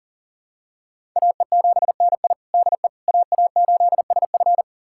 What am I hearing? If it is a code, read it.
Morse code, heard as a keyed tone, about 40 wpm, 700 Hz. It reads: AE7NI DE AA8SF
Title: USA Callsign DE Callsign - 5 Characters - 40wpm